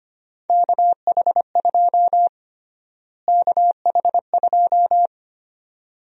Morse code, heard as a keyed tone, about 25 wpm, 700 Hz. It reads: KH2 KH2
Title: USA Call Sign Prefixes - Repeated 2x - 25wpm